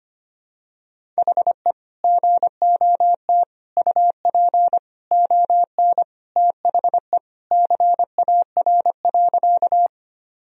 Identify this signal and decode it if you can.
Morse code, heard as a keyed tone, about 25 wpm, 700 Hz. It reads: HE GOT UP ON THE CAR.
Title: Sentences from Top 1000 Words - Review Part 1 - 25wpm